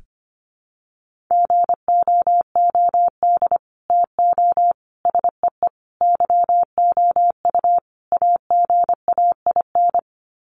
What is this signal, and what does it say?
Morse code, heard as a keyed tone, about 25 wpm, 700 Hz. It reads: GOOD TO SEE YOU AGAIN
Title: Sets of 5 Words Encore2 25wpm